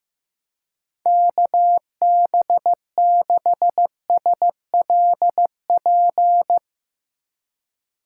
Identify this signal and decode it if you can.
Morse code, heard as a keyed tone, about 15 wpm, 700 Hz. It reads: KB6SLP
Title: USA Call Signs - 6 Characters - 4x Character Spacing - 15wpm